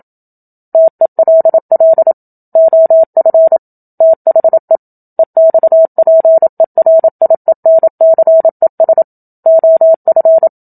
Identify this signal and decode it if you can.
Morse code, heard as a keyed tone, about 27 wpm, 650 Hz. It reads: TELL OF THE EXPERIENCES OF
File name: Book Two / 01-Chapter 1 - 27 WPM